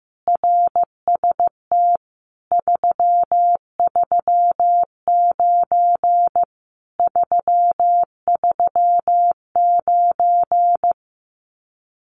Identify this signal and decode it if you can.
Morse code, heard as a keyed tone, about 15 wpm, 700 Hz. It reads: RST 339 339
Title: Common QSO Phrases Encore 15wpm